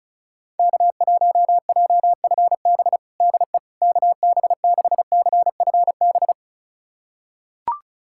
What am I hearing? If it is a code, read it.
Morse code, heard as a keyed tone, about 34 wpm, 700 Hz. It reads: K1JFB DE KB6CFB
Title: USA Callsign DE Callsign 35wpm